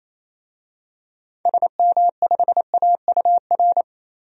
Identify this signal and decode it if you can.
Morse code, heard as a keyed tone, about 28 wpm, 700 Hz. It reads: SM5AUR
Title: DXCC-100 Call Signs - 6 Characters - 28wpm